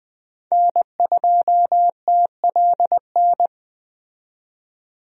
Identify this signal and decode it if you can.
Morse code, heard as a keyed tone, about 20 wpm, 700 Hz. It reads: N2TLN
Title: USA Call Signs - 5 Characters - 20wpm